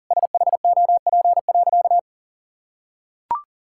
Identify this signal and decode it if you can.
Morse code, heard as a keyed tone, about 40 wpm, 700 Hz. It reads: SHOP.
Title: Sentences from Top 1600 Words 40wpm